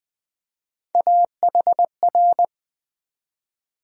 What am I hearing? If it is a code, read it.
Morse code, heard as a keyed tone, about 20 wpm, 700 Hz. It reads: AHR
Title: USA Call Sign Postfixes - 3 Letters 20wpm